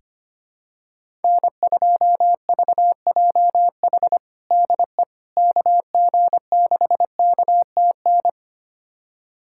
Morse code, heard as a keyed tone, about 25 wpm, 700 Hz. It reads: N2VJH DE KG6KTN
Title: USA Callsign DE Callsign 25wpm